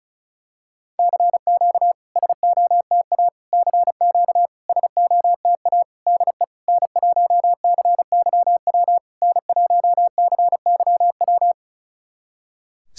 Morse code, heard as a keyed tone, about 36 wpm, 700 Hz. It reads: CQ SOTA CQ SOTA DE N1CYW N1CYW
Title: SOTA QSO Practice 35wpm